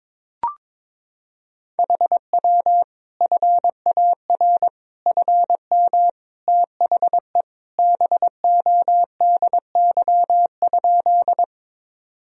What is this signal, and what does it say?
Morse code, heard as a keyed tone, about 21 wpm, 700 Hz. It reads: HW FAR FM THE BODY?
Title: Sentences from Top 400 Words and Abbreviations 22wpm